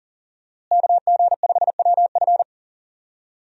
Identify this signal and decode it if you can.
Morse code, heard as a keyed tone, about 40 wpm, 700 Hz. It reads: KG5WF
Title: USA Call Signs - 5 Characters - Repeated 3x - 40wpm